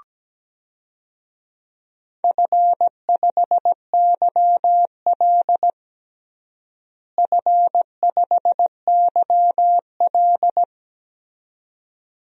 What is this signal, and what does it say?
Morse code, heard as a keyed tone, about 17 wpm, 700 Hz. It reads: F5YL F5YL
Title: DXCC-100 Call Signs - 4 Characters - Repeated 2x - 17wpm